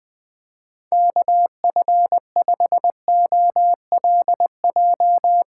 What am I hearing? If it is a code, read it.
Morse code, heard as a keyed tone, about 20 wpm, 700 Hz. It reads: KF5OLJ
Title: USA Call Signs - 6 Characters - 3x Character Spacing - 20wpm